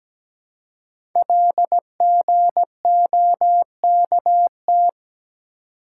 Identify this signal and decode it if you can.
Morse code, heard as a keyed tone, about 17 wpm, 700 Hz. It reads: LGOKT
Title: Code Groups - 5 Letters - 17wpm